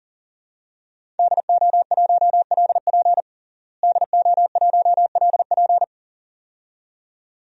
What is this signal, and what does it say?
Morse code, heard as a keyed tone, about 40 wpm, 700 Hz. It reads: DO1LP DO1LP
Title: DXCC-100 Call Signs - 5 Characters - Repeated 2x - 40wpm